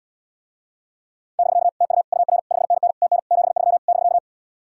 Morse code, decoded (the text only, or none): <BK> TU UR 5NN TN 73 <BK>